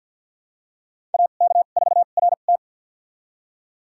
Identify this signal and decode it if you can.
Morse code, heard as a keyed tone, about 49 wpm, spaced as for 35 wpm, 690 Hz. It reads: AK4RT
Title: USA Call Signs - 2x Character Spacing - 50wpm